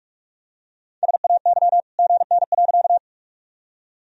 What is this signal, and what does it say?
Morse code, heard as a keyed tone, about 46 wpm, 700 Hz. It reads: SAY GN.
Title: Sentences from Top 300 Words and Abbreviations 45wpm